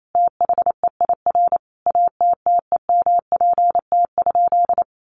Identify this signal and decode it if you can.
Morse code, heard as a keyed tone, about 28 wpm, 700 Hz. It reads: THEIR ATTEMPT?
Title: Sentences from Top 1000 Words 28wpm